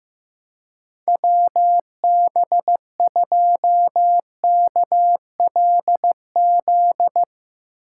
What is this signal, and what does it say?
Morse code, heard as a keyed tone, about 15 wpm, 700 Hz. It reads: WB2KLZ
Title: Speed Racing - USA 2x3 Call Signs 15wpm